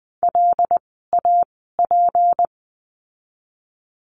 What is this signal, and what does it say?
Morse code, heard as a keyed tone, about 20 wpm, spaced as for 14 wpm, 700 Hz. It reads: LAP
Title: 153 Lesson 25f - Letter p - Words 20wpm